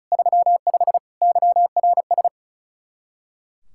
Morse code, heard as a keyed tone, about 45 wpm, 700 Hz. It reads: 35 YRS